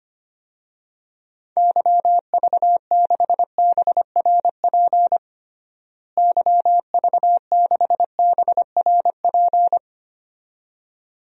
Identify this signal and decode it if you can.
Morse code, heard as a keyed tone, about 25 wpm, 700 Hz. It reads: YV6BRP YV6BRP
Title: DXCC-100 Call Signs - 6 Characters - Repeated 2x - 25wpm